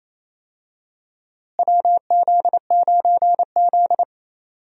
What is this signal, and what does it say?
Morse code, heard as a keyed tone, about 28 wpm, 700 Hz. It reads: WZ9Z
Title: USA Call Signs - 4 Characters - Repeated 3x - 28wpm